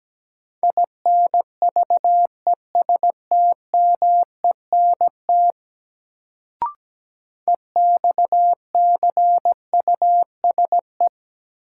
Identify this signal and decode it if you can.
Morse code, heard as a keyed tone, about 17 wpm, 700 Hz. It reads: INVESTMENT EXCUSE
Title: Sentences from Top 700 Words - New Word Drill - 17wpm